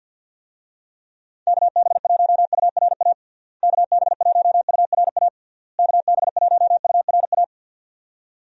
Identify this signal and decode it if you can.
Morse code, heard as a keyed tone, about 48 wpm, 700 Hz. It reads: KB1URA KB1URA KB1URA
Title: USA Call Signs - 6 Characters - Repeated 3x - 50wpm